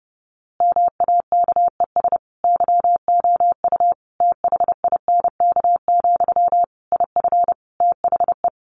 Morse code, heard as a keyed tone, about 30 wpm, 700 Hz. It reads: MAKES YOU THINK, IF THE
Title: A Sound Of Thunder 30wpm